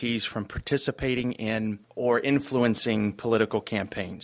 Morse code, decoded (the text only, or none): none